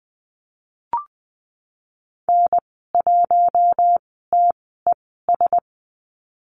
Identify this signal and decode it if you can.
Morse code, heard as a keyed tone, about 20 wpm, spaced as for 15 wpm, 700 Hz. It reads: N1TES
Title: 055 Lesson 10g - Letter r - Call Signs 20wpm